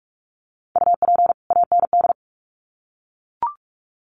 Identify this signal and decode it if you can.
Morse code, heard as a keyed tone, about 44 wpm, 700 Hz. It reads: UP AND
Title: Sets of 2 Words 45wpm